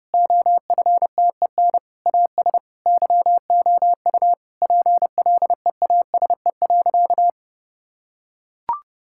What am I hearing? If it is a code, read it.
Morse code, heard as a keyed tone, about 32 wpm, 700 Hz. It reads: OFTEN AS YOU PLEASE.
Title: Sentences from Top 1600 Words 30wpm